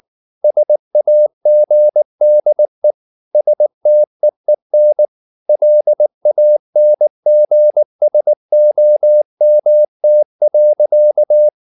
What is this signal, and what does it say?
Morse code, heard as a keyed tone, about 19 wpm, 600 Hz. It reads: SAGDE STEEN LANGSOMT.
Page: Ferie 19 wpm